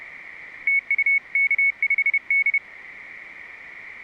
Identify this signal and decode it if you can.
Morse code, heard as a keyed tone, about 30 wpm, 2200 Hz. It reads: TU K5D